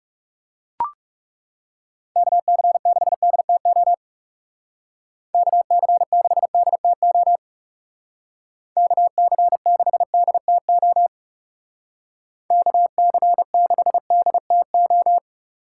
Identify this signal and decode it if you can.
Morse code, heard as a keyed tone, about 36 wpm, 700 Hz. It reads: KC6DTO KC6DTO KC6DTO KC6DTO
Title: Speed Racing - USA 2x3 Call Signs 30wpm